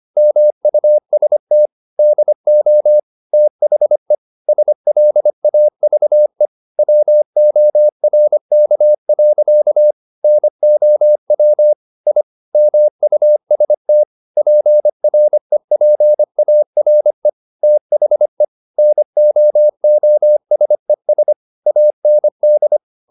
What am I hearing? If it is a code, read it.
Morse code, heard as a keyed tone, about 25 wpm, 600 Hz. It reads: MUST DO THE SLAVE WORK. NOW I MUST PREPARE THE NOOSES AND